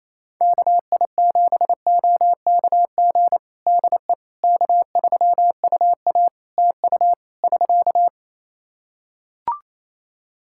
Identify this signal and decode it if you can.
Morse code, heard as a keyed tone, about 27 wpm, 700 Hz. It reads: KI7OKG DE K3UA TU <SK>
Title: Practice QSO - Exchange 4 - Ending - 28wpm